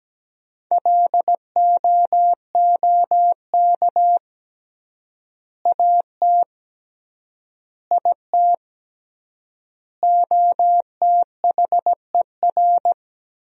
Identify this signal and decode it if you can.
Morse code, heard as a keyed tone, about 17 wpm, 700 Hz. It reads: LOOK AT IT OTHER
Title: 3x Word-Spacing - Sentences from Top 200 Words - 17wpm